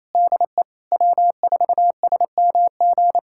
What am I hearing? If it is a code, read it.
Morse code, heard as a keyed tone, about 28 wpm, 690 Hz. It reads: DE W4SMG
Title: SOTA QSO Practice 28wpm